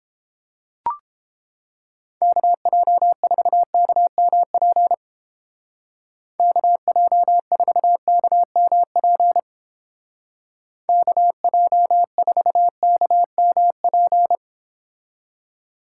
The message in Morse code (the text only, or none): KJ4KMP KJ4KMP KJ4KMP